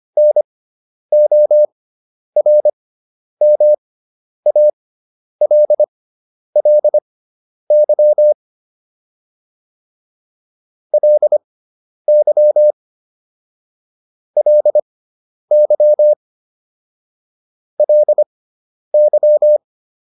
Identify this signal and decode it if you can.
Morse code, heard as a keyed tone, about 25 wpm, spaced as for 10 wpm, 600 Hz. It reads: NORMALLY LY LY LY